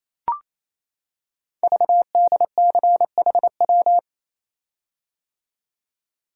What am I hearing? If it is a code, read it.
Morse code, heard as a keyed tone, about 29 wpm, 700 Hz. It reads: VDCHW
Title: Code Groups - 5 Letters - 28wpm